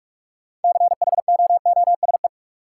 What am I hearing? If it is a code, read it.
Morse code, heard as a keyed tone, about 47 wpm, 700 Hz. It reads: CHOOSE
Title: Sentences from Top 700 Words - New Word Drill - 45wpm